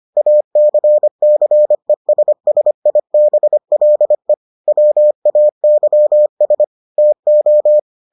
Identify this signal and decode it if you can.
Morse code, heard as a keyed tone, about 25 wpm, 600 Hz. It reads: ACCESSIBLE WAYS TO